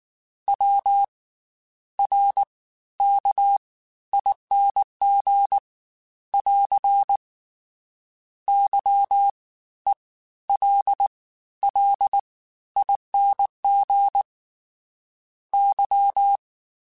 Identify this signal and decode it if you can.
Morse code, heard as a keyed tone, about 19 wpm, spaced as for 12 wpm, 800 Hz. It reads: WRKING <AR> YELLING Y